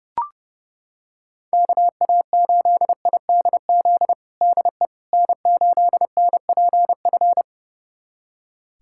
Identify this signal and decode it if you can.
Morse code, heard as a keyed tone, about 30 wpm, 700 Hz. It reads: KA8IDZ DE N8NPF
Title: USA Callsign DE Callsign 30wpm